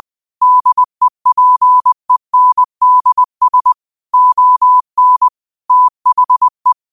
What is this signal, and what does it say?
Morse code, heard as a keyed tone, about 20 wpm, 1000 Hz. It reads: DEPENDS ON THE